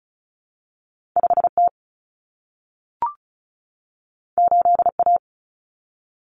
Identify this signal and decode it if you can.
Morse code, heard as a keyed tone, about 36 wpm, 700 Hz. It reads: 5T 8A
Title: Common Cut Numbers - 2-Digit 35wpm